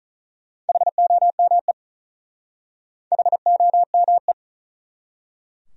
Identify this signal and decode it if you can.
Morse code, heard as a keyed tone, about 43 wpm, 700 Hz. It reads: HOME HOME
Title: Speed-Racing Top 200 Words 35wpm